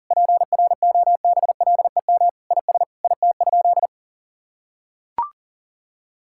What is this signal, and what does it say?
Morse code, heard as a keyed tone, about 40 wpm, 700 Hz. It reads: PROBLEM IS IT?
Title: Sentences from Top 1000 Words - Review Part 6 - 40wpm